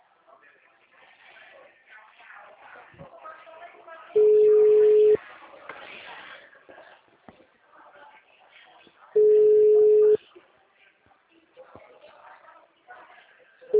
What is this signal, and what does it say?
Ringback tone (Europe, 425 Hz), with speech.